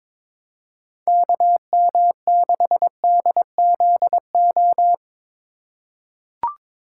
Morse code, heard as a keyed tone, about 22 wpm, 700 Hz. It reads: KM6DZO
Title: USA Call Signs - 6 Characters - 3x Character Spacing - 22wpm